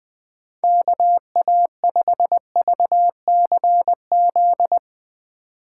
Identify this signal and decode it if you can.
Morse code, heard as a keyed tone, about 20 wpm, 700 Hz. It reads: KA5VCZ
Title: USA Call Signs - 2x Character Spacing - 20wpm